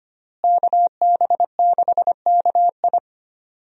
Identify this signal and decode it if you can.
Morse code, heard as a keyed tone, about 25 wpm, 700 Hz. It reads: KB6KI
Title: USA Call Signs - 5 Characters - Repeated 3x - 25wpm